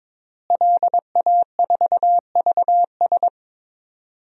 Morse code, heard as a keyed tone, about 22 wpm, 700 Hz. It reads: LA4VS
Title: DXCC-100 Call Signs - 5 Characters - Repeated 2x - 22wpm